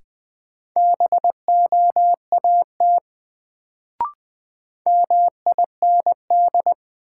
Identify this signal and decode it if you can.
Morse code, heard as a keyed tone, about 20 wpm, 700 Hz. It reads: BOAT MIND
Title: Sentences from Top 500 Words - New Word Drill - 20wpm